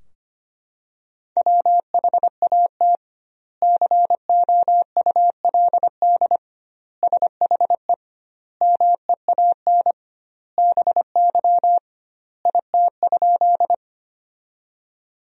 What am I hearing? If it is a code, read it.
Morse code, heard as a keyed tone, about 25 wpm, 700 Hz. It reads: WHAT COULD SHE MEAN BY IT?
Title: 2x Word-Spacing - Sentences from Top 200 Words - 25wpm